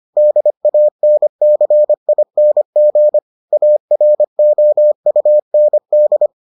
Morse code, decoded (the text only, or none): DANCING AROUND